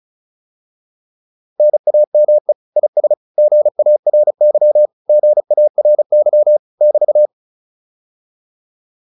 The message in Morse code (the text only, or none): NAME IS GARY GARY <BT>